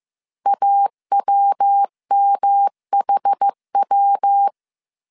Morse code, heard as a keyed tone, about 15 wpm, 800 Hz. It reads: AWMHW